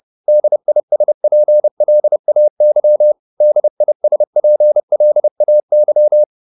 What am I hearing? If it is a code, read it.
Morse code, heard as a keyed tone, about 30 wpm, 600 Hz. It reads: DISPLAY DISPLAY